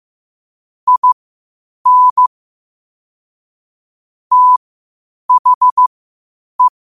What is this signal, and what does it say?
Morse code, heard as a keyed tone, about 15 wpm, spaced as for 8 wpm, 1000 Hz. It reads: IN THE
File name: qotdfiles / 8wpm